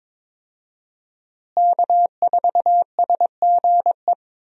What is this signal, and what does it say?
Morse code, heard as a keyed tone, about 22 wpm, 700 Hz. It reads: K4SGE
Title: USA Call Signs - 5 Characters - Repeated 3x - 22wpm